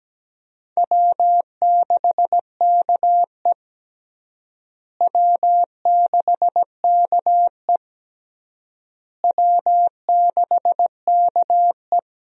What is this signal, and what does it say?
Morse code, heard as a keyed tone, about 17 wpm, 700 Hz. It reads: W6KE W6KE W6KE
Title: USA Call Signs - 4 Characters - Repeated 3x - 17wpm